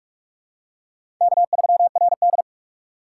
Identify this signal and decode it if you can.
Morse code, heard as a keyed tone, about 46 wpm, 700 Hz. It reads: K3RD
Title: USA Call Signs - 4x Character Spacing - 45wpm